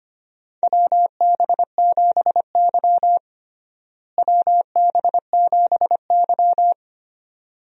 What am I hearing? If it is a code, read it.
Morse code, heard as a keyed tone, about 25 wpm, 700 Hz. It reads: WB7Y WB7Y
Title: USA Call Signs - 4 Characters - Repeated 2x - 25wpm